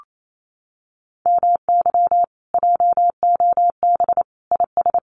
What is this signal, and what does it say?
Morse code, heard as a keyed tone, about 28 wpm, 700 Hz. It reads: MY JOB IS
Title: Common QSO Phrases Encore 28wpm